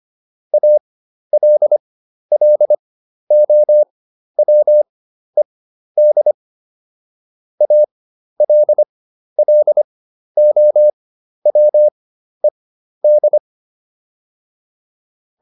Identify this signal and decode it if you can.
Morse code, heard as a keyed tone, about 25 wpm, spaced as for 12 wpm, 600 Hz. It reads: ALLOWED ALLOWED